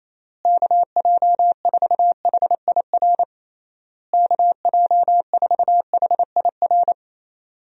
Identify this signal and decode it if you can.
Morse code, heard as a keyed tone, about 28 wpm, 700 Hz. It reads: KJ4HIR KJ4HIR
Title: USA Call Signs - 6 Characters - Repeated 3x - 28wpm